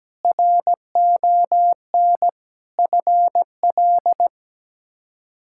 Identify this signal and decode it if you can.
Morse code, heard as a keyed tone, about 17 wpm, 700 Hz. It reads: RON FL
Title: K1USN SST QSO Practice - 17wpm